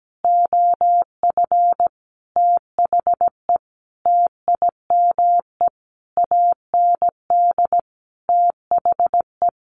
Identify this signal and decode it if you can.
Morse code, heard as a keyed tone, about 17 wpm, 700 Hz. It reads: OF THE TIME AND THE
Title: Sentences from Top 1500 Words 17wpm